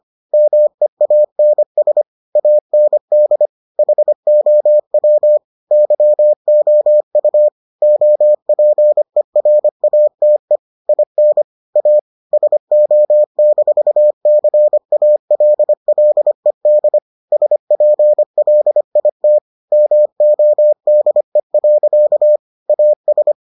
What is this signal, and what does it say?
Morse code, heard as a keyed tone, about 25 wpm, 600 Hz. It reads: MEANS AND HOW YOU OPERATE IN A SO-CALLED SPLIT MODE. AS